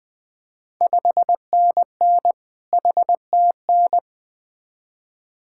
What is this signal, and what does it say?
Morse code, heard as a keyed tone, about 20 wpm, 700 Hz. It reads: 5NN HTN
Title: Ontario QSO Party Exchange 20wpm